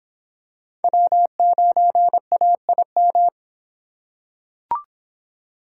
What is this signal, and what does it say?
Morse code, heard as a keyed tone, about 24 wpm, 700 Hz. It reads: W9AIM